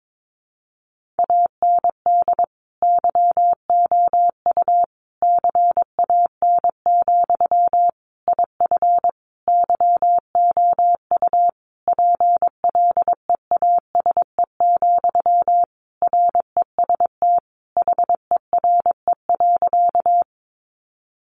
Morse code, heard as a keyed tone, about 22 wpm, 700 Hz. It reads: AND YOU CAN, IF YOU PLEASE, REST HERE.
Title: Sentences from Top 1500 Words 22wpm